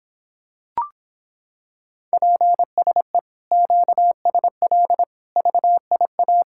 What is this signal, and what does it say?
Morse code, heard as a keyed tone, about 26 wpm, 700 Hz. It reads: PSE QSL VIA